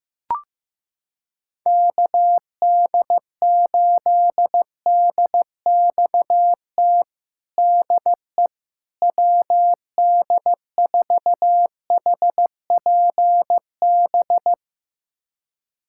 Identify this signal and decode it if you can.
Morse code, heard as a keyed tone, about 15 wpm, 700 Hz. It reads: KD8DXT DE WD4HPB
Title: USA Callsign DE Callsign - 6 Characters - 15wpm